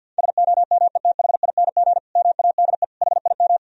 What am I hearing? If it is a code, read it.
Morse code, heard as a keyed tone, about 52 wpm, 700 Hz. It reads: SOMETHING MADE HIM
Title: Sentences from Top 1000 Words - Review Part 1 - 50wpm